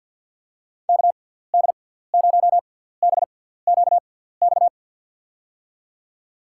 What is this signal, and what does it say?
Morse code, heard as a keyed tone, about 51 wpm, 700 Hz. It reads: K D 0 B Q X
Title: USA Call Signs - 6 Characters - 6x Character Spacing - 50wpm